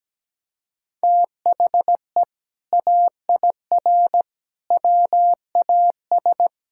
Morse code, heard as a keyed tone, about 17 wpm, 700 Hz. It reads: THE AIR WAS
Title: Sentences from Top 1600 Words 17wpm